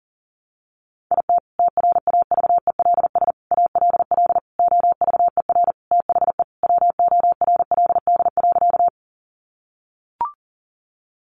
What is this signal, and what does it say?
Morse code, heard as a keyed tone, about 40 wpm, 700 Hz. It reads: IT TRAVELS ALL OVER THE WORLD.
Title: Additional Easy Phrases 40wpm